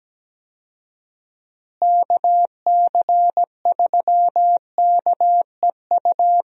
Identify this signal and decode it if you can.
Morse code, heard as a keyed tone, about 17 wpm, 700 Hz. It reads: KC3KEU
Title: USA Call Signs - 6 Characters - Repeated 3x - 17wpm